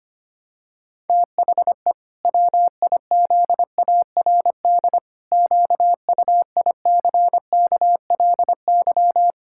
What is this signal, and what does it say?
Morse code, heard as a keyed tone, about 25 wpm, 700 Hz. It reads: THE WIZARD QUICKLY
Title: Pangram Challenge v2 - 25wpm